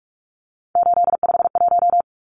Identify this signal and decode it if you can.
Morse code, heard as a keyed tone, about 44 wpm, 700 Hz. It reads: 851
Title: Numbers - 3-Digit 45wpm